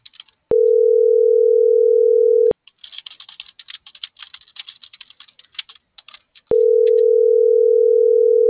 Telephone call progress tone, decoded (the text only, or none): ringback tone